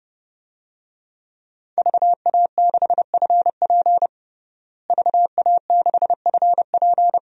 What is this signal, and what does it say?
Morse code, heard as a keyed tone, about 30 wpm, 700 Hz. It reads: VA6FP VA6FP
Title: DXCC-100 Call Signs - 5 Characters - Repeated 2x - 30wpm